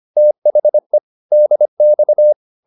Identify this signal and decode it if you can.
Morse code, heard as a keyed tone, about 25 wpm, 600 Hz. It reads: THE DX